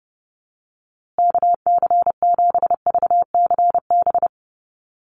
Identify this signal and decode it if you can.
Morse code, heard as a keyed tone, about 30 wpm, 700 Hz. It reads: KC7VCB
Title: USA Call Signs Repeated 2x 30wpm